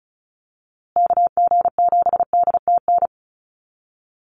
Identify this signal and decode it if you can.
Morse code, heard as a keyed tone, about 34 wpm, 700 Hz. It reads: KG7DTN
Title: USA Call Signs Repeated 2x 35wpm